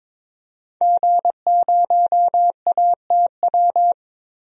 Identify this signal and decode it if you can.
Morse code, heard as a keyed tone, about 22 wpm, 700 Hz. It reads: G0ATW
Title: DXCC-100 Call Signs - 5 Characters - Repeated 2x - 22wpm